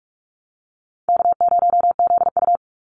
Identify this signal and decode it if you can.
Morse code, heard as a keyed tone, about 44 wpm, 700 Hz. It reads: K0ZU
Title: Easy Call Signs - 45wpm